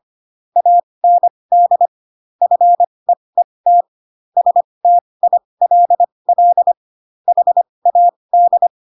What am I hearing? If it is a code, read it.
Morse code, heard as a keyed tone, about 25 wpm, spaced as for 20 wpm, 700 Hz. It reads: AND FEET STILL HAD
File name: Pinocchio / Pinocchio ch3 15 25 ik1hge